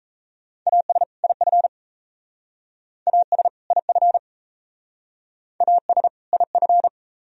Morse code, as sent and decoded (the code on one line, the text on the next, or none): .- ... / .. ..-. / .- ... / .. ..-. / .- ... / .. ..-.
AS IF AS IF AS IF